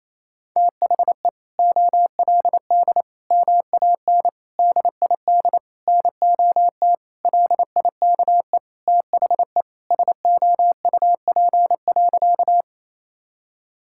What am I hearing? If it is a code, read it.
Morse code, heard as a keyed tone, about 28 wpm, 700 Hz. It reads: THE OLD MAN DID NOT LIKE THE SOUP.
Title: Sentences from Top 1000 Words - Review Part 6 - 28wpm